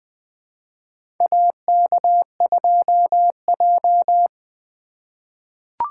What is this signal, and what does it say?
Morse code, heard as a keyed tone, about 20 wpm, 700 Hz. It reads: AK2J
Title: USA Call Signs - 4 Characters - 3x Character Spacing - 20wpm